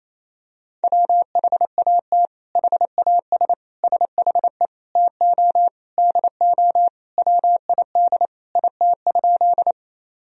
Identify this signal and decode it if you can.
Morse code, heard as a keyed tone, about 27 wpm, 700 Hz. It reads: WHAT HAS SHE TO DO WID IT?
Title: Sentences from Top 100 Words and Abbreviations 28wpm